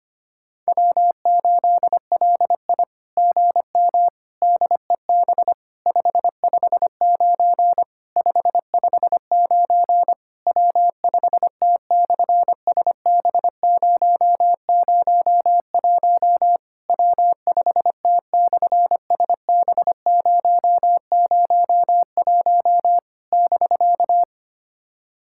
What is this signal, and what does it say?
Morse code, heard as a keyed tone, about 24 wpm, 700 Hz. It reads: W8LI GM DEB 559 559 W5T/SB001 W5T/SB001 <BK>
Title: SOTA QSO Practice 25wpm